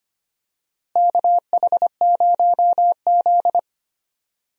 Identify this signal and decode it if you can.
Morse code, heard as a keyed tone, about 25 wpm, 700 Hz. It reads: KH0Z
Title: USA Call Signs - 4 Characters - Repeated 2x - 25wpm